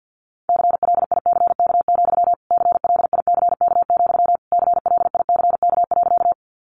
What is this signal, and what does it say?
Morse code, heard as a keyed tone, about 50 wpm, 700 Hz. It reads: CLICK, CLICK, CLICK.
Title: Sentences from Top 1100 Words 50wpm